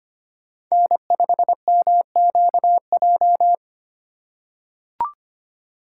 Morse code, heard as a keyed tone, about 25 wpm, 700 Hz. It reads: N5MQJ
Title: USA Call Signs - 5 Characters - 2x Character Spacing - 25wpm